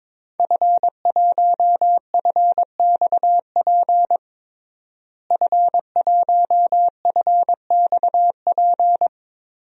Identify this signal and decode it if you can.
Morse code, heard as a keyed tone, about 21 wpm, 700 Hz. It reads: F1FXP F1FXP
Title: DXCC-100 Call Signs - 5 Characters - Repeated 3x - 22wpm